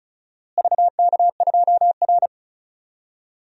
Morse code, heard as a keyed tone, about 34 wpm, 700 Hz. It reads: VK2R
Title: DXCC-100 Call Signs - 4 Characters - Repeated 3x - 35wpm